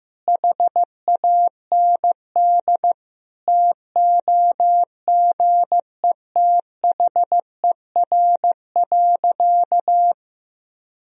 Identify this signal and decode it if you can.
Morse code, heard as a keyed tone, about 16 wpm, 700 Hz. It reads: HAND TOGETHER.
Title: Sentences from Top 1000 Words - Review Part 3 - 15wpm